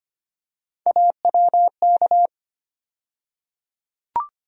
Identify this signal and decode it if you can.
Morse code, heard as a keyed tone, about 25 wpm, 700 Hz. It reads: AWK
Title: USA Call Sign Postfixes - 3 Letters 25wpm